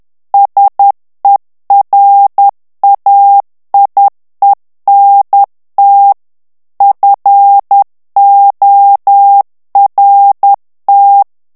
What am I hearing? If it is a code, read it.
Morse code, heard as a keyed tone, about 10 wpm, 800 Hz. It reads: SERAIENT FORT
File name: FRANCAIS TEXTE CLAIR 2 VITESSE 10 MOTS MINUTES